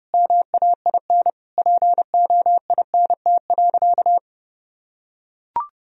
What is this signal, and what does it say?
Morse code, heard as a keyed tone, about 30 wpm, 700 Hz. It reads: MAIN POINT.
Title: Sentences from Top 1000 Words - Review Part 2 - 30wpm